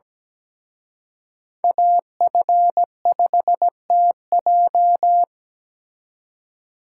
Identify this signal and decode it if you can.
Morse code, heard as a keyed tone, about 20 wpm, 700 Hz. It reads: AF5TJ
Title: Speed Racing - USA 2x2 Call Signs 17wpm